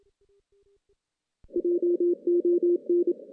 Morse code, heard as a keyed tone, about 27 wpm, 350 Hz. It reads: JON
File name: 2016 JAN CW NAQP / 2016 Jan naqp k5ot - 0-4